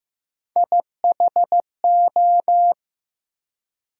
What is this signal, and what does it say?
Morse code, heard as a keyed tone, about 15 wpm, 700 Hz. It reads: IHO
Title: USA Call Sign Postfixes - 3 Letters 15wpm